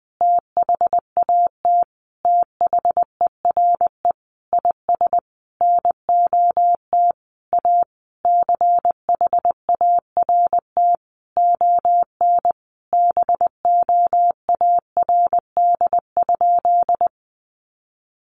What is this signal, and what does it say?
Morse code, heard as a keyed tone, about 20 wpm, 700 Hz. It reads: THAT THERE IS NOT A CHART ON BOARD?
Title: Sentences from Top 1300 Words 20wpm